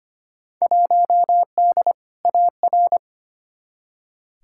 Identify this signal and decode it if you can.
Morse code, heard as a keyed tone, about 25 wpm, 700 Hz. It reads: 1D AR
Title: Field Day Exchanges - 25wpm